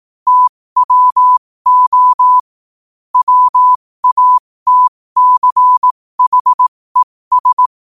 Morse code, heard as a keyed tone, about 18 wpm, spaced as for 15 wpm, 1000 Hz. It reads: TWO WATCHES
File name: qotdfiles / 15wpm